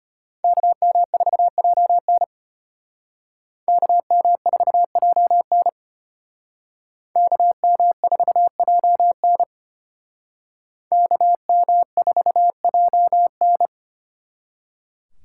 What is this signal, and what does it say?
Morse code, heard as a keyed tone, about 33 wpm, 700 Hz. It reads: KM4JN KM4JN KM4JN KM4JN